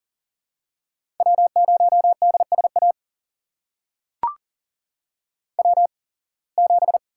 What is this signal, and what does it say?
Morse code, heard as a keyed tone, about 37 wpm, spaced as for 25 wpm, 700 Hz. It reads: W0DSA W7
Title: USA Call Signs - 5 Characters - 8x Character Spacing - 40wpm